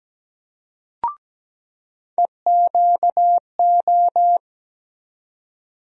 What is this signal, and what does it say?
Morse code, heard as a keyed tone, about 17 wpm, 700 Hz. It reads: EQO
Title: USA Call Sign Postfixes - 3 Letters 17wpm